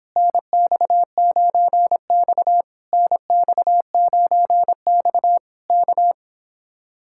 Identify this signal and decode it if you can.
Morse code, heard as a keyed tone, about 30 wpm, 700 Hz. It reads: NX9X NX9X K